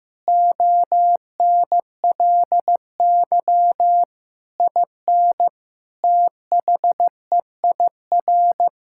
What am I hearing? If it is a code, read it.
Morse code, heard as a keyed tone, about 16 wpm, 700 Hz. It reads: ONLY IN THEIR
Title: Sentences from Top 1000 Words - Review Part 5 - 15wpm